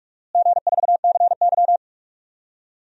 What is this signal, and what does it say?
Morse code, heard as a keyed tone, about 46 wpm, 700 Hz. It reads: G4CY
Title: DXCC-100 Call Signs - 4 Characters - Repeated 2x - 45wpm